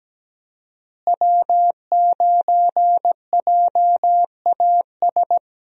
Morse code, heard as a keyed tone, about 17 wpm, 700 Hz. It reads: W9JAS
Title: USA Call Signs - 5 Characters - Repeated 3x - 17wpm